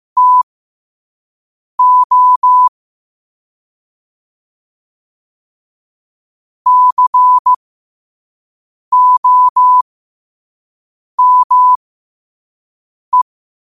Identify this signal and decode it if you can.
Morse code, heard as a keyed tone, about 15 wpm, spaced as for 5 wpm, 1000 Hz. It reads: TO COME
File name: qotdfiles / 5wpm